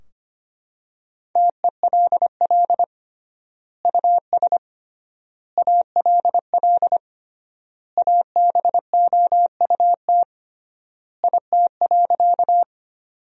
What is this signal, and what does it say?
Morse code, heard as a keyed tone, about 25 wpm, 700 Hz. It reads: TELL US ALL ABOUT IT.
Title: 3x Word-Spacing - Sentences from Top 200 Words - 25wpm